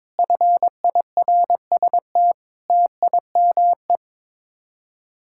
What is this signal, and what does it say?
Morse code, heard as a keyed tone, about 22 wpm, 700 Hz. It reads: FIRST TIME
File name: practice / 22wpm / Sets of 6 Words - 22wpm